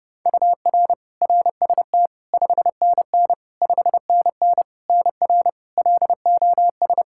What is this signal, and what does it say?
Morse code, heard as a keyed tone, about 31 wpm, 700 Hz. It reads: UR RST 5NN 5NN NR LOS
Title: K1USN SST QSO Practice 30wpm